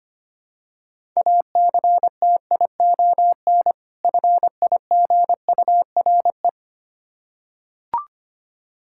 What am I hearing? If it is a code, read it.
Morse code, heard as a keyed tone, about 25 wpm, 700 Hz. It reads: ACTION FIGURE
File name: practice / 25wpm / Difficult Words with Hints - 25wpm